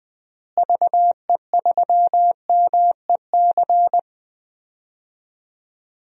Morse code, heard as a keyed tone, about 20 wpm, 700 Hz. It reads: VE3MEC
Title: DXCC-100 Call Signs - 6 Characters - Repeated 3x - 20wpm